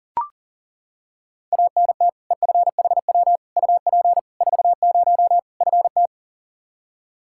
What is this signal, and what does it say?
Morse code, heard as a keyed tone, about 40 wpm, 700 Hz. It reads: ANT EFHW UP 40 FT
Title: Practice QSO - Exchange 2 - Info - 40wpm